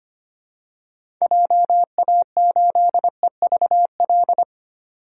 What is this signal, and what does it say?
Morse code, heard as a keyed tone, about 25 wpm, 700 Hz. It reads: JA8EVL
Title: DXCC-100 Call Signs - 6 Characters - 25wpm